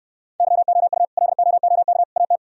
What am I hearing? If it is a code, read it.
Morse code, heard as a keyed tone, about 72 wpm, 700 Hz. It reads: YOU LOOK AT